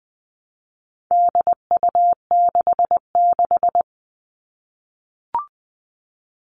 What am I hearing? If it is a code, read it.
Morse code, heard as a keyed tone, about 20 wpm, 700 Hz. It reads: DU66
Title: All Cut Numbers - 4-Digit 20wpm